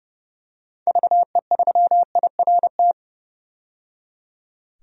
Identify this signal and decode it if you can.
Morse code, heard as a keyed tone, about 29 wpm, 700 Hz. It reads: VE3IRT
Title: DXCC-100 Call Signs - 6 Characters - 30wpm